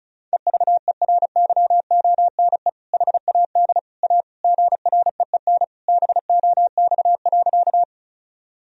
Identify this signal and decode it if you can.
Morse code, heard as a keyed tone, about 34 wpm, 700 Hz. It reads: EVERYONE HAD A GREEN BOX.
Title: Sentences from Top 1000 Words - Review Part 2 - 35wpm